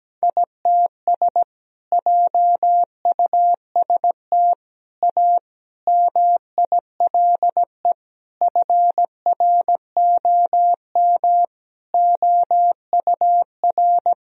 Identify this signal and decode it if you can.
Morse code, heard as a keyed tone, about 17 wpm, 700 Hz. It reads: ITS JUST A MILE FROM OUR
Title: Sentences from Top 1000 Words - Review Part 1 - 17wpm